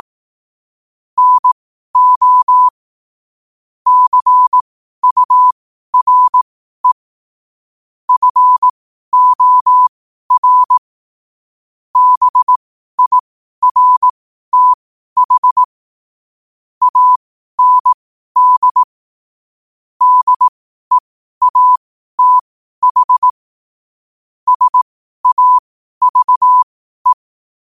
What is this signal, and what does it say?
Morse code, heard as a keyed tone, about 18 wpm, spaced as for 12 wpm, 1000 Hz. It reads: NO CURE FOR BIRTH AND DEATH SAVE
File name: qotdfiles / 12wpm